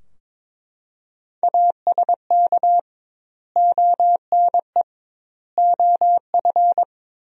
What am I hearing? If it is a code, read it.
Morse code, heard as a keyed tone, about 22 wpm, 700 Hz. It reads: ASK ONE OF
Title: 2x Word-Spacing - Sentences from Top 300 Words - 22wpm